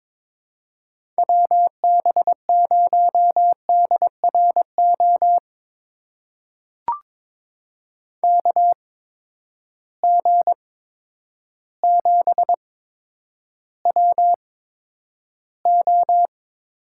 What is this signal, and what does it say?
Morse code, heard as a keyed tone, about 21 wpm, spaced as for 10 wpm, 700 Hz. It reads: WB0DRO KG7WO
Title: USA Call Signs - 6 Characters - 8x Character Spacing - 22wpm